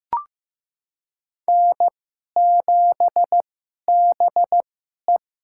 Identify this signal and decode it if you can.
Morse code, heard as a keyed tone, about 15 wpm, spaced as for 11 wpm, 700 Hz. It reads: N7BE
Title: USA Call Signs - 2x Character Spacing - 15wpm